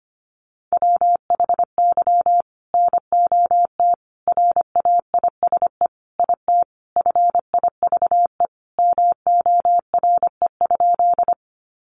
Morse code, heard as a keyed tone, about 25 wpm, 700 Hz. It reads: WHY NOT RAISE IT FIVE MORE?
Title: Sentences from Top 1400 Words 25wpm